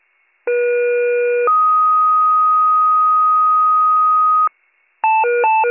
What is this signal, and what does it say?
Two-tone page, heard as a one-second tone, then a three-second tone, about 480 then 1250 Hz. It is heard from 0.5 to 4.5 s.